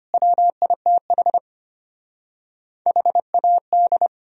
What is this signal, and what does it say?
Morse code, heard as a keyed tone, about 33 wpm, 700 Hz. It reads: WITH HAD